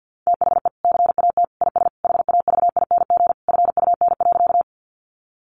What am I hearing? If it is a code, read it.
Morse code, heard as a keyed tone, about 50 wpm, 700 Hz. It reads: THE CAT IS HAVING FUN.
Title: Easy Phrases 50wpm